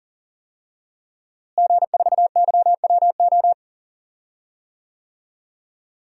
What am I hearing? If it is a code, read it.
Morse code, heard as a keyed tone, about 40 wpm, 700 Hz. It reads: G4YWO
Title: DXCC-100 Call Signs - 5 Characters - Repeated 3x - 40wpm